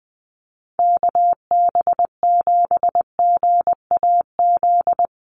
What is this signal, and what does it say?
Morse code, heard as a keyed tone, about 19 wpm, 700 Hz. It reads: KB7GAZ
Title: USA Call Signs Repeated 2x 20wpm